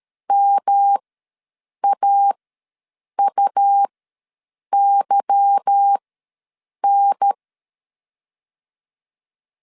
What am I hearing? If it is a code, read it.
Morse code, heard as a keyed tone, about 13 wpm, spaced as for 7 wpm, 800 Hz. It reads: MAUYN
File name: LEZIONI / 8 YJ45 / 8 Y ws 12wpm 3409